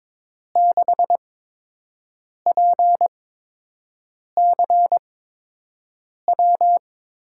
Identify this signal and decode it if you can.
Morse code, heard as a keyed tone, about 22 wpm, spaced as for 6 wpm, 700 Hz. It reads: 6PCW
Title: USA Call Signs - 5 Characters - 8x Character Spacing - 22wpm